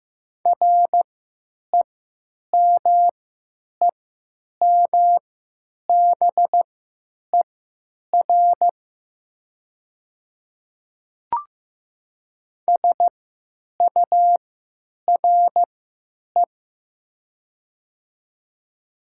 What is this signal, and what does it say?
Morse code, heard as a keyed tone, about 15 wpm, spaced as for 9 wpm, 700 Hz. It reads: REMEMBER SURE
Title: 3x Character Spacing - Top 500 Words - 15wpm